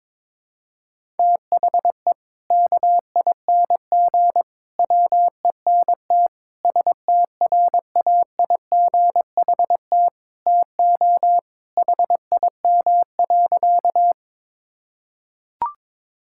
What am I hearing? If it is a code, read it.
Morse code, heard as a keyed tone, about 22 wpm, 700 Hz. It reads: THE KING WENT STRAIGHT TO HIM.
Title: Sentences from Top 1000 Words - Review Part 6 - 22wpm